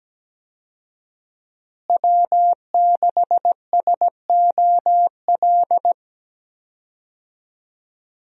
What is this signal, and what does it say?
Morse code, heard as a keyed tone, about 17 wpm, 700 Hz. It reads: W6SOL
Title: USA Call Signs - 5 Characters - Repeated 3x - 17wpm